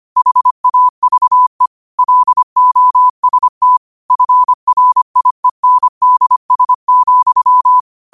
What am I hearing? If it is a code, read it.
Morse code, heard as a keyed tone, about 25 wpm, 1000 Hz. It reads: HAVE LOST FRIENDS,
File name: qotdfiles / 25wpm